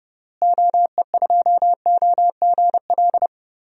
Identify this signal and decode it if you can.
Morse code, heard as a keyed tone, about 32 wpm, 700 Hz. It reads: OE2OGL
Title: DXCC-100 Call Signs - 6 Characters - Repeated 3x - 30wpm